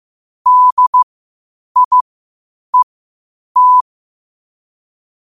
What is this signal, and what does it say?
Morse code, heard as a keyed tone, about 15 wpm, spaced as for 8 wpm, 1000 Hz. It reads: DIET
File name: qotdfiles / 8wpm